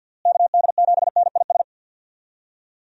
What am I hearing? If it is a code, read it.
Morse code, heard as a keyed tone, about 51 wpm, 700 Hz. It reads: KD7NIS
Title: USA Call Signs - 6 Characters - Repeated 2x - 50wpm